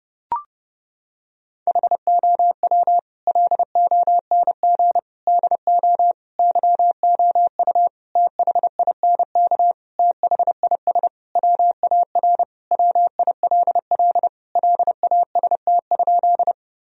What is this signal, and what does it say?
Morse code, heard as a keyed tone, about 29 wpm, 700 Hz. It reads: HOW LONG DO YOU THINK THIS WAR WILL LAST?
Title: Sentences from Top 1000 Words - Review Part 2 - 30wpm